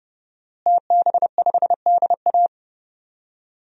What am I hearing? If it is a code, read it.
Morse code, heard as a keyed tone, about 30 wpm, 700 Hz. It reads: TB5DA
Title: DXCC-100 Call Signs - 5 Characters - Repeated 2x - 30wpm